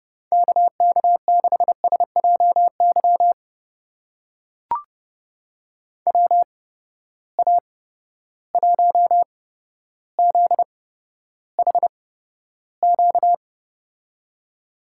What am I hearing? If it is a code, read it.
Morse code, heard as a keyed tone, about 30 wpm, spaced as for 13 wpm, 700 Hz. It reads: KK6SJY WA1ZHQ
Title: USA Call Signs - 6 Characters - 8x Character Spacing - 30wpm